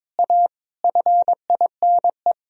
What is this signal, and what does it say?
Morse code, heard as a keyed tone, about 22 wpm, 690 Hz. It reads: A FINE
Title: Sentences from Top 1000 Words - Review Part 3 - 22wpm